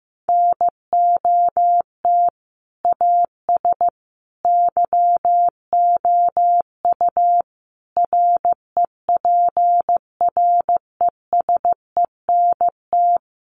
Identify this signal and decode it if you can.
Morse code, heard as a keyed tone, about 15 wpm, 700 Hz. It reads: NOT AS YOU REPRESENT
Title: Sentences from Top 1200 Words 15wpm